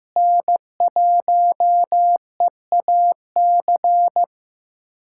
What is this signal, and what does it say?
Morse code, heard as a keyed tone, about 15 wpm, 700 Hz. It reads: N1EAC
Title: USA Callsign DE Callsign - 5 Characters - 15wpm